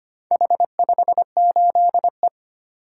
Morse code, heard as a keyed tone, about 25 wpm, 700 Hz. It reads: H58E
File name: practice / 25wpm / Four Letter-Number Combinations - 25wpm